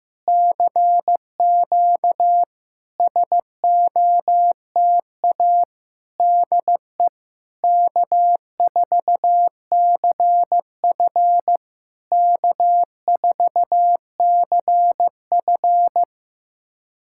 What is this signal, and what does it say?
Morse code, heard as a keyed tone, about 15 wpm, 700 Hz. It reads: CQ SOTA DE K4CF K4CF
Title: SOTA QSO Practice 15wpm